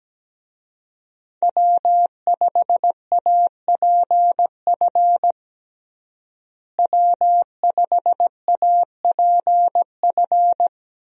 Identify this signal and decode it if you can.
Morse code, heard as a keyed tone, about 17 wpm, 700 Hz. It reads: W5APF W5APF
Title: USA Call Signs - 5 Characters - Repeated 3x - 17wpm